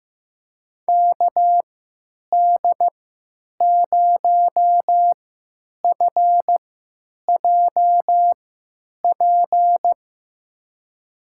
Morse code, heard as a keyed tone, about 15 wpm, spaced as for 9 wpm, 700 Hz. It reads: KD0FJP
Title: USA Call Signs - 6 Characters - 3x Character Spacing - 15wpm